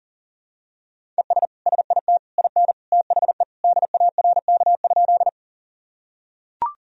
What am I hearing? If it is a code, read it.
Morse code, heard as a keyed tone, about 40 wpm, 700 Hz. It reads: ES SIT IN THE DARK?
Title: Sentences from Top 500 Words and Abbreviations 40wpm